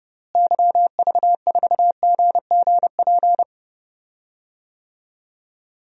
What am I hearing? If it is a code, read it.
Morse code, heard as a keyed tone, about 29 wpm, 700 Hz. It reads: YV4GGP
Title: DXCC-100 Call Signs - 6 Characters - Repeated 2x - 30wpm